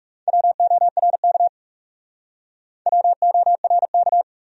Morse code, heard as a keyed tone, about 38 wpm, 700 Hz. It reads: WORK WORK